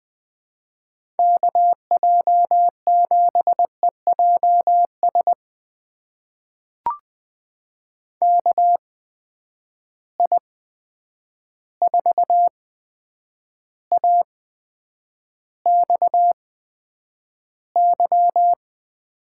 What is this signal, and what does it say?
Morse code, heard as a keyed tone, about 21 wpm, spaced as for 9 wpm, 700 Hz. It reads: KJ7EJS KI4AXY
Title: USA Call Signs - 8x Character Spacing - 20wpm